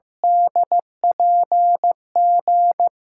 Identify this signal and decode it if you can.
Morse code, heard as a keyed tone, about 15 wpm, 700 Hz. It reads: DPG
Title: USA Call Signs - 6x Character Spacing - 15wpm